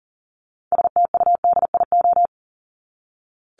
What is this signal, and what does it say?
Morse code, heard as a keyed tone, about 40 wpm, 700 Hz. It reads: STUDIO
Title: Top 1001-1250 Words 40wpm